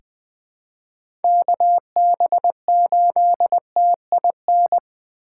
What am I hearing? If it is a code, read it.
Morse code, heard as a keyed tone, about 20 wpm, 700 Hz. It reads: KB8TIN
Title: USA Call Signs - 6 Characters - 2x Character Spacing - 20wpm